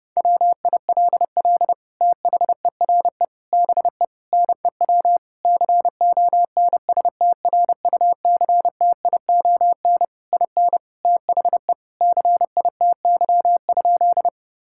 Morse code, heard as a keyed tone, about 32 wpm, 700 Hz. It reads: WILL THERE BE NEW CONSTRUCTION IN THE CITY?
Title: Sentences from Top 1600 Words 30wpm